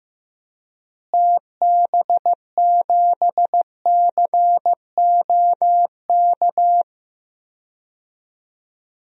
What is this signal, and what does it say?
Morse code, heard as a keyed tone, about 15 wpm, 700 Hz. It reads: TB7COK
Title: DXCC-100 Call Signs - 6 Characters - Repeated 3x - 15wpm